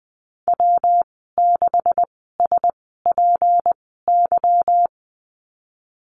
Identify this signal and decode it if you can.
Morse code, heard as a keyed tone, about 20 wpm, spaced as for 15 wpm, 700 Hz. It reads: W6SPY
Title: 173 Lesson 28g - Number 9 - Call Signs 20wpm